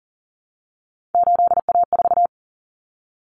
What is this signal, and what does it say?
Morse code, heard as a keyed tone, about 40 wpm, 700 Hz. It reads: 8A4
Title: Common Cut Numbers - 3-Digit 40wpm